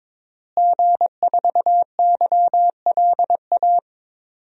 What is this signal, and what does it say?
Morse code, heard as a keyed tone, about 22 wpm, 700 Hz. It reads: G4YLA
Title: DXCC-100 Call Signs - 5 Characters - Repeated 3x - 22wpm